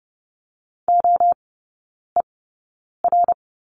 Morse code, heard as a keyed tone, about 30 wpm, spaced as for 9 wpm, 700 Hz. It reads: OER
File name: Lesson 17e - Letter u - Three Characters 30wpm